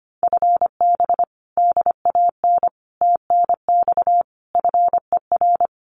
Morse code, heard as a keyed tone, about 25 wpm, 700 Hz. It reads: FB DAN TNX FER
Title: Common QSO Phrases Encore 25wpm